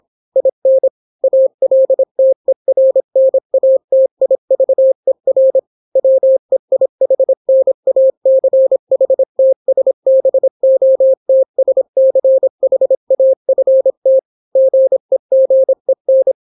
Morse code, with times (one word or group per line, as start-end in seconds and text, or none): 0.4-0.9 IN
1.2-5.6 ALTERNATIVER
6.0-14.2 WEIHNACHTSBOTSCHAFT
14.6-16.3 GEGEN